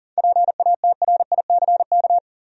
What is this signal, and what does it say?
Morse code, heard as a keyed tone, about 40 wpm, 700 Hz. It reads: PATRICK
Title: USA Common Names - 40wpm